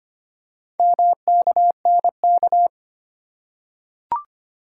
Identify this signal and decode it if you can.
Morse code, heard as a keyed tone, about 25 wpm, 700 Hz. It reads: MKNK
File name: practice / 25wpm / Four Letter-Number Combinations - 25wpm